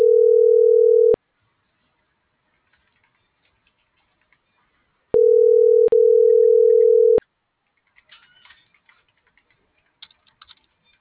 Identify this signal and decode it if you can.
Ringback tone (North America), 440 + 480 Hz.